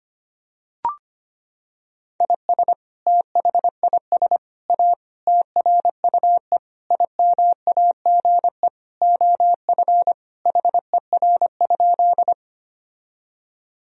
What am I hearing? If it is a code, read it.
Morse code, heard as a keyed tone, about 25 wpm, 700 Hz. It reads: IS THIS A TRUE IMAGE OF HER?
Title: Sentences from Top 1000 Words - Review Part 4 - 25wpm